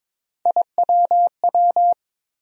Morse code, heard as a keyed tone, about 22 wpm, 700 Hz. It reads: IWW
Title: USA Call Sign Postfixes - 3 Letters 22wpm